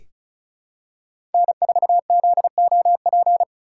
Morse code, heard as a keyed tone, about 36 wpm, 700 Hz. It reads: N4ZOP
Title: USA Call Signs - 2x Character Spacing - 35wpm